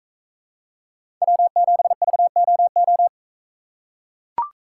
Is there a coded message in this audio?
Yes: Morse code from 1.2 to 3.1 s, heard as a keyed tone, about 42 wpm, 700 Hz. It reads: W7VOO